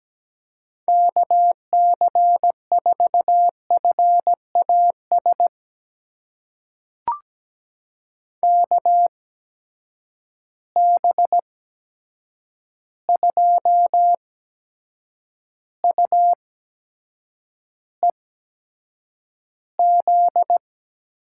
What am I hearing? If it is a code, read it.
Morse code, heard as a keyed tone, about 17 wpm, spaced as for 7 wpm, 700 Hz. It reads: KC4FAS KB2UEZ
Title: USA Call Signs - 6 Characters - 8x Character Spacing - 17wpm